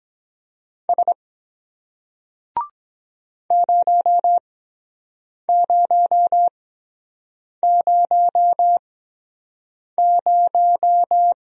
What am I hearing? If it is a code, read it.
Morse code, heard as a keyed tone, about 23 wpm, spaced as for 7 wpm, 700 Hz. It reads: S 0000